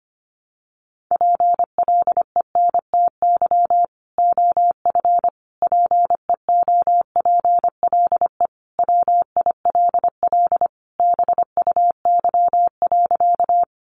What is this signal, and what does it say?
Morse code, heard as a keyed tone, about 25 wpm, 700 Hz. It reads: PLENTY OF PEOPLE WILL BUY.
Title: Sentences from Top 1300 Words 25wpm